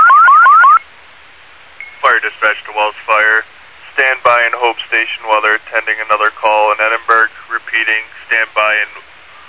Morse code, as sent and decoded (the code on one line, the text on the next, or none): none